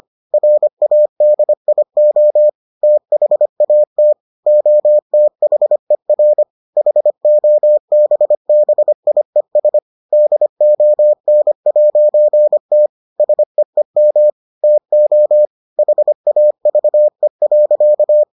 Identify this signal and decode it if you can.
Morse code, heard as a keyed tone, about 25 wpm, 600 Hz. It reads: RADIO THAT OTHER HOBBIES DON'T SEEM TO HAVE.